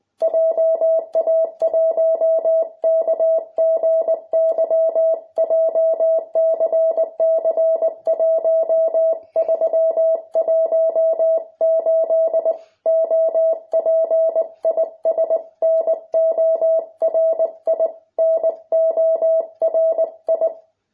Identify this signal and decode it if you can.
Morse code, heard as a keyed tone, about 20 wpm, 650 Hz. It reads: JA1KGY JCC1318 OP IS NORI NORI